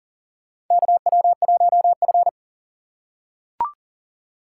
Morse code, heard as a keyed tone, about 40 wpm, 700 Hz. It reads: KW1F
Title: USA Call Signs - 4 Characters - 4x Character Spacing - 40wpm